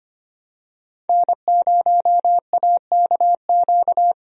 Morse code, heard as a keyed tone, about 25 wpm, 700 Hz. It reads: N0AKQ
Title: USA Call Signs - 5 Characters - Repeated 3x - 25wpm